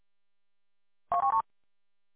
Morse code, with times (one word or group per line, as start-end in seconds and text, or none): none